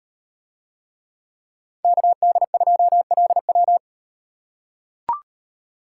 Morse code, heard as a keyed tone, about 38 wpm, 700 Hz. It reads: KD2LW